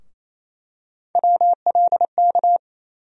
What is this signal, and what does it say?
Morse code, heard as a keyed tone, about 28 wpm, 700 Hz. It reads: WLK
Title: USA Call Sign Postfixes - 3 Letters 28wpm